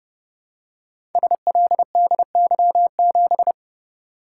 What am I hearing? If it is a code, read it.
Morse code, heard as a keyed tone, about 29 wpm, 700 Hz. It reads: SLDY7
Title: Code Groups - 5 Letter:Numbers - 30wpm